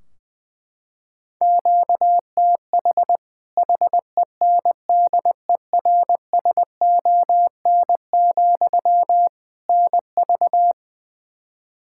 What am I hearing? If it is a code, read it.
Morse code, heard as a keyed tone, about 20 wpm, 700 Hz. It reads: QTH HENDERSON, NV
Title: Practice QSO - Exchange 1 - Beginning - 20wpm